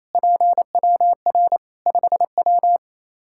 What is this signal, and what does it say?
Morse code, heard as a keyed tone, about 28 wpm, 700 Hz. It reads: PWR 5W
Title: Practice QSO - Exchange 2 - Info - 28wpm